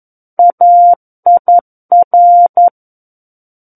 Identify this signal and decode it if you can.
Morse code, heard as a keyed tone, about 11 wpm, 700 Hz. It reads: AIR
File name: Book One / 02-Chapter 2 - 11 WPM